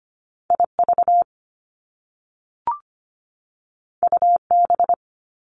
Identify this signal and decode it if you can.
Morse code, heard as a keyed tone, about 25 wpm, 700 Hz. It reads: IV UB
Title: Top Two Letter Combinations in Words 25wpm